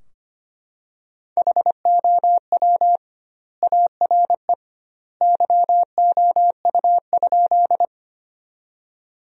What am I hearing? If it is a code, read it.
Morse code, heard as a keyed tone, about 25 wpm, 700 Hz. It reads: HOW ARE YOU?
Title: 2x Word-Spacing - Sentences from Top 100 Words - 25wpm